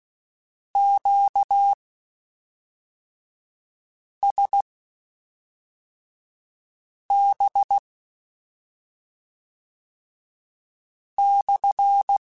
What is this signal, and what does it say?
Morse code, heard as a keyed tone, about 16 wpm, spaced as for 3 wpm, 800 Hz. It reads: QSB/
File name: Q Calls / Q Calls-mp3s / Q Calls SPSL1 @5